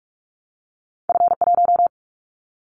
Morse code, heard as a keyed tone, about 44 wpm, 700 Hz. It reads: F1
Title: Callsign Prefixes for Top 50 DXCC - 45wpm